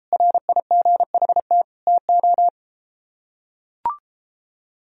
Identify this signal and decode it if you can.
Morse code, heard as a keyed tone, about 32 wpm, 700 Hz. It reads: RIGHT TO